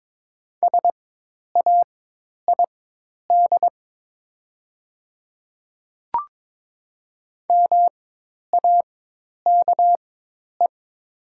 Morse code, heard as a keyed tone, about 22 wpm, spaced as for 10 wpm, 700 Hz. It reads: SAID MAKE
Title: 4x Character Spacing - Top 200 Words - 22wpm